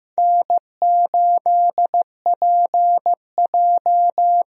Morse code, heard as a keyed tone, about 15 wpm, 700 Hz. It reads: N8PJ
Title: USA Call Signs - 4 Characters - 15wpm